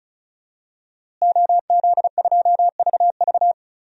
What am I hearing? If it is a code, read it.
Morse code, heard as a keyed tone, about 36 wpm, 700 Hz. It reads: OZ2VV
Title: DXCC-100 Call Signs - 5 Characters - Repeated 2x - 35wpm